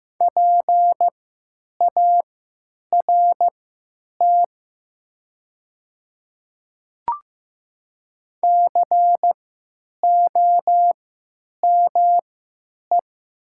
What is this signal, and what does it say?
Morse code, heard as a keyed tone, about 15 wpm, spaced as for 9 wpm, 700 Hz. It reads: PART COME
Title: 3x Character Spacing - Top 200 Words - 15wpm